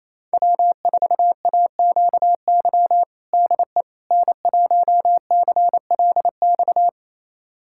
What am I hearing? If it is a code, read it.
Morse code, heard as a keyed tone, about 28 wpm, 700 Hz. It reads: W4AQY DE N1CLX
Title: USA Callsign DE Callsign - 5 Characters - 28wpm